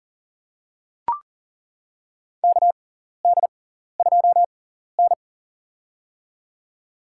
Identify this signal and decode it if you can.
Morse code, heard as a keyed tone, about 40 wpm, spaced as for 14 wpm, 700 Hz. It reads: KD2N
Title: USA Call Signs - 4 Characters - 6x Character Spacing - 40wpm